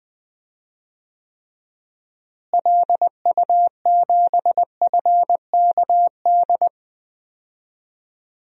Morse code, heard as a keyed tone, about 20 wpm, 700 Hz. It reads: LU7FKD
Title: DXCC-100 Call Signs - 6 Characters - Repeated 2x - 20wpm